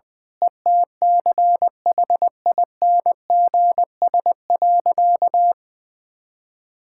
Morse code, heard as a keyed tone, about 20 wpm, 700 Hz. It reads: ETCHINGS.
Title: Pangram Challenge v2 - 20wpm